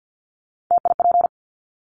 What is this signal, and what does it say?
Morse code, heard as a keyed tone, about 50 wpm, 700 Hz. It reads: TIP
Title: Top 1501-1750 Words 50wpm